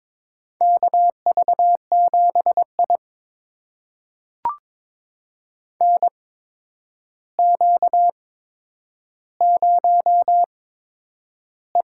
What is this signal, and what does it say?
Morse code, heard as a keyed tone, about 22 wpm, spaced as for 9 wpm, 700 Hz. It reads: KV7I NQ0E
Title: USA Call Signs - 4 Characters - 8x Character Spacing - 22wpm